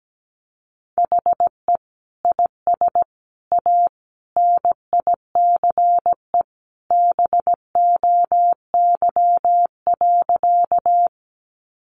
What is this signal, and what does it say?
Morse code, heard as a keyed tone, about 17 wpm, 700 Hz. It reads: HE IS A NICE BOY.
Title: Sentences from Top 1100 Words 17wpm